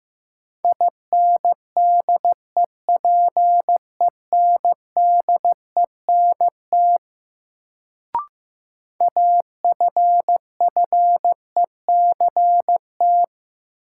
Morse code, heard as a keyed tone, about 15 wpm, 700 Hz. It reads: INDEPENDENT AFFECT
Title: Sentences from Top 900 Words - New Word Drill - 15wpm